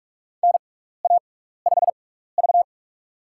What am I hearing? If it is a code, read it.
Morse code, heard as a keyed tone, about 47 wpm, spaced as for 16 wpm, 700 Hz. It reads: NA5V
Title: USA Call Signs - 6x Character Spacing - 45wpm